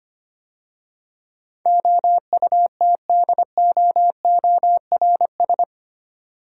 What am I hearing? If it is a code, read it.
Morse code, heard as a keyed tone, about 25 wpm, 700 Hz. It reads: OUTDOORS
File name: 8 Letter Words - First 3 Letters Given - 25wpm